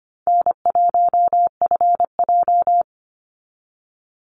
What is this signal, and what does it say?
Morse code, heard as a keyed tone, about 25 wpm, 700 Hz. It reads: N1FJ
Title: NA SOTA Activator Callsigns 25wpm